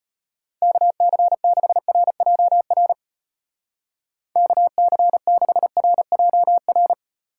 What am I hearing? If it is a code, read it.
Morse code, heard as a keyed tone, about 38 wpm, 700 Hz. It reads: KC6RJR KC6RJR